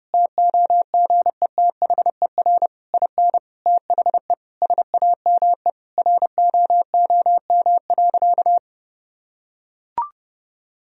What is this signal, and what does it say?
Morse code, heard as a keyed tone, about 30 wpm, 700 Hz. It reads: TOGETHER IN THE SAME ROOM.
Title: Sentences from Top 1000 Words - Review Part 3 - 30wpm